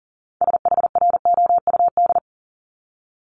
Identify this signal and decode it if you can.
Morse code, heard as a keyed tone, about 40 wpm, 700 Hz. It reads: SHROUD
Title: Multi-letter Phonemes Repeated 3x 40wpm